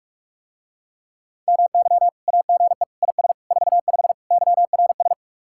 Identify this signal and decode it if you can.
Morse code, heard as a keyed tone, about 44 wpm, 700 Hz. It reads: MY AGE IS 45 YRS